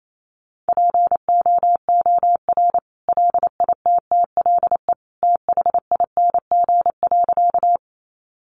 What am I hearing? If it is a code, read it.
Morse code, heard as a keyed tone, about 28 wpm, 700 Hz. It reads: POOR LITTLE THING.
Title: Sentences from Top 1000 Words 28wpm